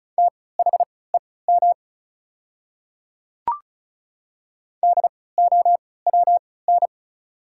Morse code, heard as a keyed tone, about 36 wpm, spaced as for 20 wpm, 690 Hz. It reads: THEM DOWN
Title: 3x Character Spacing - Top 100 Words - 35wpm